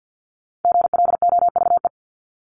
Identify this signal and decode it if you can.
Morse code, heard as a keyed tone, about 50 wpm, 700 Hz. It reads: GLOVE
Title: Top 1751-2000 Words 50wpm